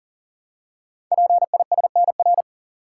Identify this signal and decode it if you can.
Morse code, heard as a keyed tone, about 42 wpm, 690 Hz. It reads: PISNR
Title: Code Groups - 5 Letters - 40wpm